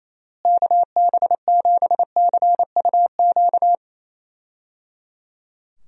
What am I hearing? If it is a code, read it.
Morse code, heard as a keyed tone, about 28 wpm, 700 Hz. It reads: KB7CUQ
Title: USA Call Signs - 6 Characters - Repeated 3x - 28wpm